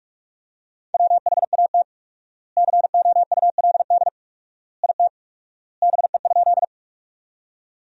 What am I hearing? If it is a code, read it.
Morse code, heard as a keyed tone, about 47 wpm, 700 Hz. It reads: WHAT COULD IT BE?
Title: 4x Word-Spacing - Sentences from Top 100 Words - 45wpm